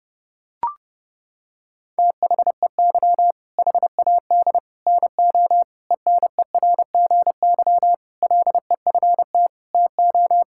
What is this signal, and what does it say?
Morse code, heard as a keyed tone, about 30 wpm, 700 Hz. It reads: THEY HAD NO ENERGY LEFT TO
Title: Sentences from Top 1000 Words - Review Part 6 - 30wpm